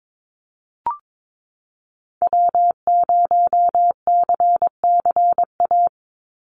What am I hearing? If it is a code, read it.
Morse code, heard as a keyed tone, about 22 wpm, 700 Hz. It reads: W0CCA
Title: NA SOTA Activator Callsigns 22wpm